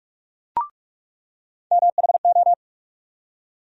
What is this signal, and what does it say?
Morse code, heard as a keyed tone, about 44 wpm, 700 Hz. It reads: MHO
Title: USA Call Sign Postfixes - 3 Letters 45wpm